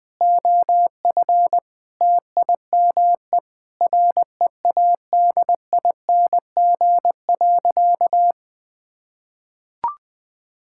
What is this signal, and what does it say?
Morse code, heard as a keyed tone, about 20 wpm, 700 Hz. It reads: OF TIME READING.
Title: Sentences with 5-Word Phrases - 20wpm